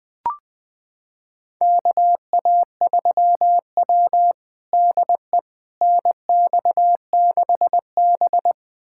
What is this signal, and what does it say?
Morse code, heard as a keyed tone, about 21 wpm, 700 Hz. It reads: KA3W DE NX6B
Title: USA Callsign DE Callsign - 4 Characters - 20wpm